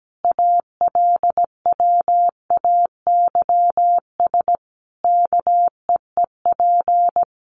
Morse code, heard as a keyed tone, about 17 wpm, 700 Hz. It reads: ALWAYS KEEP
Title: Sentences from Top 1300 Words 17wpm